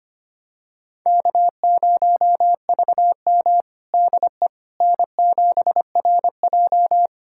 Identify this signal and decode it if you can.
Morse code, heard as a keyed tone, about 25 wpm, 700 Hz. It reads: K0VM DE N7RJ
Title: USA Callsign DE Callsign - 4 Characters - 25wpm